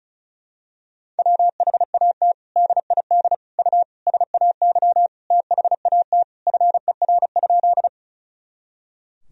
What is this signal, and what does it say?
Morse code, heard as a keyed tone, about 34 wpm, 700 Hz. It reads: WHAT DID U SAY THAT FER?
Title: Sentences from Top 200 Words and Abbreviations 35wpm